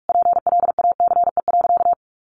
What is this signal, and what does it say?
Morse code, heard as a keyed tone, about 44 wpm, 700 Hz. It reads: PLACE.
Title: Sentences from Top 1400 Words 45wpm